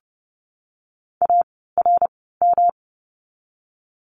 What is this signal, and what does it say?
Morse code, heard as a keyed tone, about 30 wpm, spaced as for 17 wpm, 700 Hz. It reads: ARM
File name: course / 30:10wpm / 108 Lesson 18f - Letter m - Words 30wpm